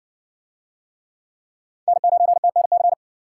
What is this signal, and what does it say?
Morse code, heard as a keyed tone, about 60 wpm, 700 Hz. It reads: N9TNC